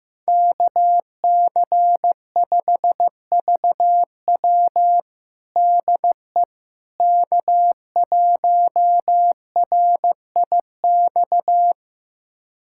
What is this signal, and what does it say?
Morse code, heard as a keyed tone, about 15 wpm, 700 Hz. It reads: KC5VW DE K1RIX
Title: USA Callsign DE Callsign 15wpm